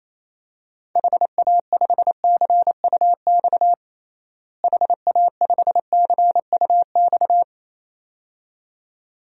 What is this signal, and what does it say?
Morse code, heard as a keyed tone, about 27 wpm, 700 Hz. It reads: HA5CUX HA5CUX
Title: DXCC-100 Call Signs - 6 Characters - Repeated 2x - 28wpm